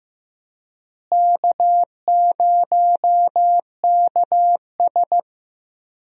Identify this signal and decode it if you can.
Morse code, heard as a keyed tone, about 15 wpm, 700 Hz. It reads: K0KS
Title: USA Call Signs - 4 Characters - Repeated 2x - 15wpm